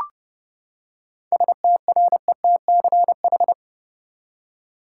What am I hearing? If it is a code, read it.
Morse code, heard as a keyed tone, about 30 wpm, 700 Hz. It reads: STRETCH
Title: Seven Letter Words - 30wpm